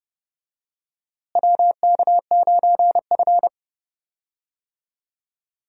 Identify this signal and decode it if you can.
Morse code, heard as a keyed tone, about 30 wpm, 700 Hz. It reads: WK9F
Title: USA Call Signs - 4 Characters - Repeated 3x - 30wpm